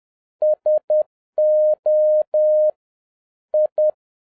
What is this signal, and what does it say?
Morse code, heard as a keyed tone, about 10 wpm, 610 Hz. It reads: SO I